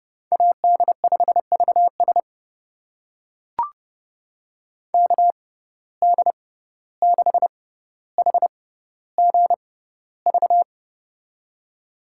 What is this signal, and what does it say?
Morse code, heard as a keyed tone, about 30 wpm, spaced as for 15 wpm, 700 Hz. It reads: AD5VS KD6HGV
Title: USA Call Signs - 6x Character Spacing - 30wpm